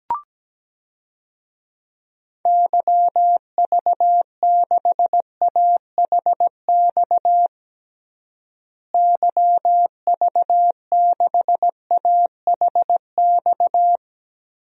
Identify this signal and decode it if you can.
Morse code, heard as a keyed tone, about 17 wpm, 700 Hz. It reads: YV6AHX YV6AHX
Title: DXCC-100 Call Signs - 6 Characters - Repeated 3x - 17wpm